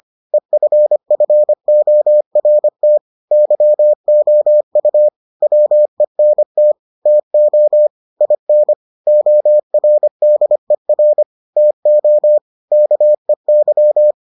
Morse code, heard as a keyed tone, about 25 wpm, 600 Hz. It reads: EFFORT YOU WENT TO IN ORDER TO KEY